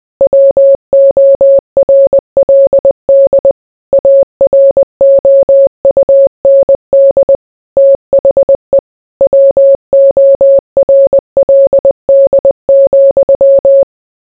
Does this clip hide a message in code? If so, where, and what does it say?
Yes: Morse code from 0.2 to 13.8 s, heard as a keyed tone, about 20 wpm, 550 Hz. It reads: WORLD AROUND THE WORLD,